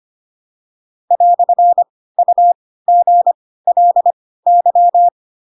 Morse code, heard as a keyed tone, about 25 wpm, spaced as for 16 wpm, 700 Hz. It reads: "UGLY
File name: Pinocchio / Pinocchio ch2 10 25 ik1hge